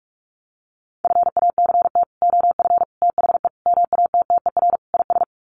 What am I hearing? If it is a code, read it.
Morse code, heard as a keyed tone, about 44 wpm, 700 Hz. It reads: FACT OF THE MATTER IS
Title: Sets of 5 Words 45wpm